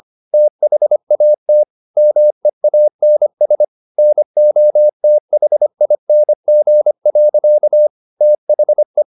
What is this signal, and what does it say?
Morse code, heard as a keyed tone, about 25 wpm, 600 Hz. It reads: THAT MEANS NOTHING. THE
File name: the old man and the sea part 1 25wpm